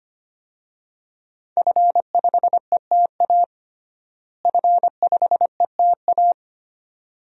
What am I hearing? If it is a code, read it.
Morse code, heard as a keyed tone, about 25 wpm, 700 Hz. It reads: F5ETA F5ETA
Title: DXCC-100 Call Signs - 5 Characters - Repeated 2x - 25wpm